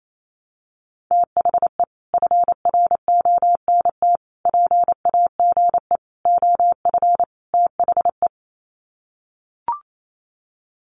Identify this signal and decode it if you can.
Morse code, heard as a keyed tone, about 28 wpm, 700 Hz. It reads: THE FRONT PAGE OF THE
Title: Sets of 5 Words Encore 28wpm